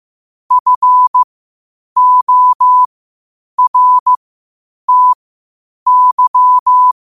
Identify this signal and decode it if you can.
Morse code, heard as a keyed tone, about 15 wpm, spaced as for 8 wpm, 1000 Hz. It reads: FORTY
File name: qotdfiles / 8wpm